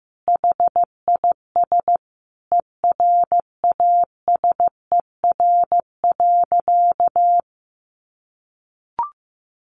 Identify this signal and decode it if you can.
Morse code, heard as a keyed tone, about 15 wpm, 700 Hz. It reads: HIS ERASER.
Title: Additional Easy Phrases 15wpm